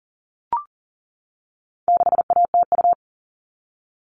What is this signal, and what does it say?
Morse code, heard as a keyed tone, about 40 wpm, 700 Hz. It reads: 6ATU
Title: All Cut Numbers - 4-Digit 40wpm